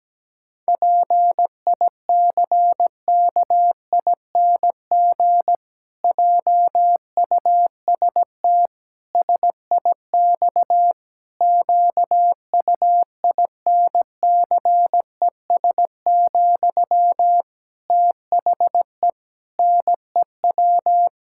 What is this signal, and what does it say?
Morse code, heard as a keyed tone, about 17 wpm, 700 Hz. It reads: PICKING JUST SIX QUINCES, THE NEW
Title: Pangram Challenge v2 - 17wpm